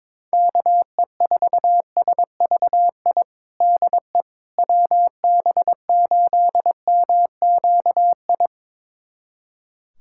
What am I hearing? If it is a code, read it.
Morse code, heard as a keyed tone, about 22 wpm, 700 Hz. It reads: KE4SVI DE WB8MQI
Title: USA Callsign DE Callsign - 6 Characters - 22wpm